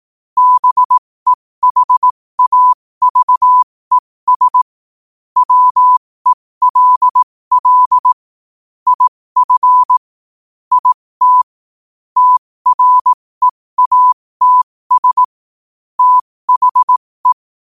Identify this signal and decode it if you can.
Morse code, heard as a keyed tone, about 18 wpm, spaced as for 15 wpm, 1000 Hz. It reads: BEHAVES WELL IF IT TREATS THE
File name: qotdfiles / 15wpm